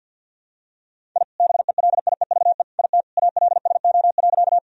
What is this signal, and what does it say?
Morse code, heard as a keyed tone, about 51 wpm, 700 Hz. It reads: I BELIEVE IT ALSO.
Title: Sentences from Top 1000 Words - Review Part 1 - 50wpm